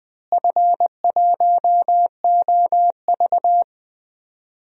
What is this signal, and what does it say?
Morse code, heard as a keyed tone, about 19 wpm, 700 Hz. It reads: F1OV
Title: DXCC-100 Call Signs - 4 Characters - Repeated 3x - 20wpm